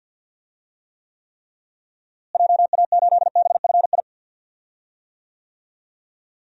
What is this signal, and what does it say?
Morse code, heard as a keyed tone, about 46 wpm, 700 Hz. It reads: JA8BFI
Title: DXCC-100 Call Signs - 6 Characters - Repeated 3x - 50wpm